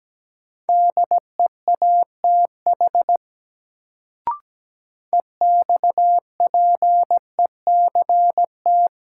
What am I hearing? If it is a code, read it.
Morse code, heard as a keyed tone, about 17 wpm, 700 Hz. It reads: DEATH EXPECT
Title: Sentences from Top 1100 Words - New Word Drill Alt - 17wpm